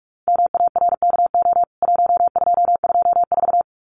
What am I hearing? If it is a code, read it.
Morse code, heard as a keyed tone, about 44 wpm, 700 Hz. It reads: MARKO 1224
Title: CWT Exchange 45wpm